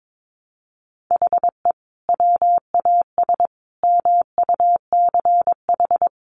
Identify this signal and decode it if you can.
Morse code, heard as a keyed tone, about 22 wpm, 700 Hz. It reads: HE WAS MUCH
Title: Sentences from Top 1200 Words 22wpm